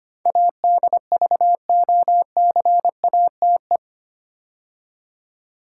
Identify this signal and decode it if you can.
Morse code, heard as a keyed tone, about 25 wpm, 700 Hz. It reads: ADVOCATE
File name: 8 Letter Words - First 3 Letters Given - 25wpm